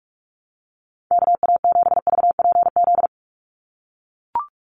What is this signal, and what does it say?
Morse code, heard as a keyed tone, about 44 wpm, 700 Hz. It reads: KA7VPZ
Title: USA Call Signs 45wpm